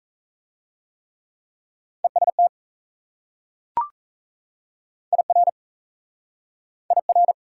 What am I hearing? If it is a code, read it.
Morse code, heard as a keyed tone, about 34 wpm, 700 Hz. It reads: EST IR IR
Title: Speed-Racing Prefixes and Suffixes 28wpm